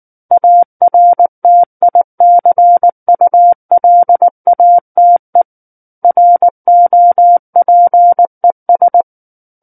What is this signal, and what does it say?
Morse code, heard as a keyed tone, about 19 wpm, 700 Hz. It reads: ARTICULATE ROPES
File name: Book One / 10-Chapter 10 - 19 WPM